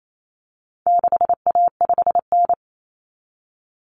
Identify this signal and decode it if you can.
Morse code, heard as a keyed tone, about 28 wpm, 700 Hz. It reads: 6A5N
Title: Common Cut Numbers - 4-Digit 28wpm